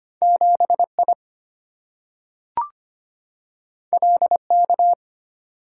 Morse code, heard as a keyed tone, about 25 wpm, 690 Hz. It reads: 7I LK
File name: Two Letter-Number Combinations - 25wpm